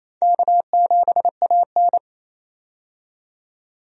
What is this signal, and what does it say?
Morse code, heard as a keyed tone, about 28 wpm, 700 Hz. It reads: K7AN
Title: USA Call Signs - 4 Characters - 28wpm